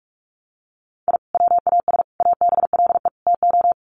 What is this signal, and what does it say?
Morse code, heard as a keyed tone, about 44 wpm, 700 Hz. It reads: I WAS ABLE TO
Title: Sets of 4 Words 45wpm